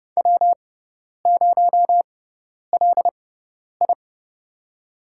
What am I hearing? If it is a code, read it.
Morse code, heard as a keyed tone, about 31 wpm, spaced as for 10 wpm, 700 Hz. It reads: W0LI
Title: USA Call Signs - 4 Characters - 6x Character Spacing - 30wpm